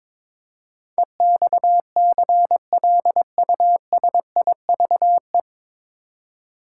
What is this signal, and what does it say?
Morse code, heard as a keyed tone, about 22 wpm, 700 Hz. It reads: EXCLUSIVE
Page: Morse Code Ninja practice 22wpm